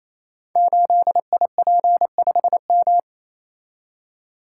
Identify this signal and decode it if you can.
Morse code, heard as a keyed tone, about 27 wpm, 700 Hz. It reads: 8IP5M
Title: Code Groups - 5 Letter:Numbers - 28wpm